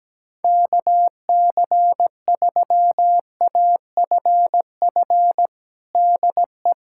Morse code, heard as a keyed tone, about 17 wpm, 700 Hz. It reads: KC3AFF DE
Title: USA Callsign DE Callsign - 6 Characters - 17wpm